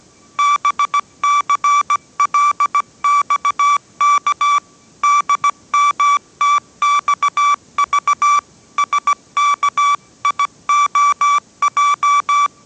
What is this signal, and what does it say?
Morse code, heard as a keyed tone, about 17 wpm, 1160 Hz. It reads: BCLXK DMTXV SKIOJ
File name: 18wpm